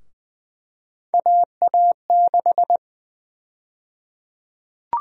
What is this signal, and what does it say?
Morse code, heard as a keyed tone, about 20 wpm, 700 Hz. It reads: AA6
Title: USA Call Sign Prefixes - 20wpm